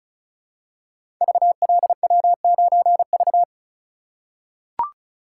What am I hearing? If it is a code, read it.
Morse code, heard as a keyed tone, about 36 wpm, 700 Hz. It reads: VLW9V
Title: Code Groups - 5 Letter:Numbers - 35wpm